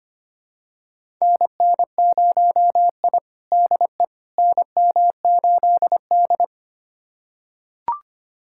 Morse code, heard as a keyed tone, about 25 wpm, 700 Hz. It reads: NN0I DE NM8D
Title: USA Callsign DE Callsign - 4 Characters - 25wpm